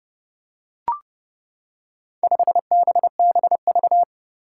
Morse code, heard as a keyed tone, about 32 wpm, 700 Hz. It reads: 5BBV
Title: Four Challenging Letter-Number Combinations 30wpm